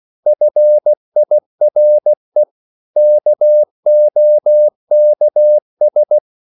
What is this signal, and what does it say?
Morse code, heard as a keyed tone, about 16 wpm, 600 Hz. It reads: FIRE KOKS
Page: Korte danske ord 16wpm